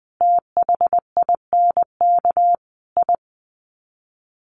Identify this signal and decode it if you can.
Morse code, heard as a keyed tone, about 20 wpm, 700 Hz. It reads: THINK I
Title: Sets of 5 Words Encore 20wpm